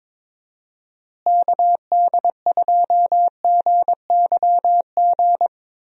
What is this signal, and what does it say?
Morse code, heard as a keyed tone, about 22 wpm, 700 Hz. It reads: KD2GYG
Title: USA Call Signs - 6 Characters - Repeated 2x - 22wpm